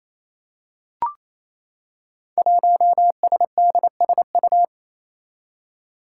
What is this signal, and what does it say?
Morse code, heard as a keyed tone, about 27 wpm, 700 Hz. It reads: 1SDSU
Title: Five Challenging Letter-Number Combinations 28wpm